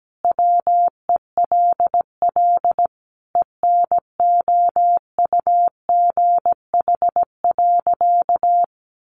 Morse code, heard as a keyed tone, about 17 wpm, 700 Hz. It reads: WELL ENOUGH.
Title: Sentences from Top 400 Words 17wpm